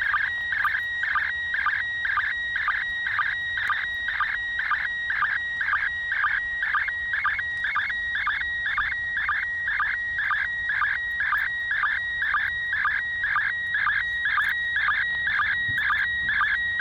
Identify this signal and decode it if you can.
SSTV (slow-scan television), PD 120.
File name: ISS 2017-12-06 145 800MHz SSTV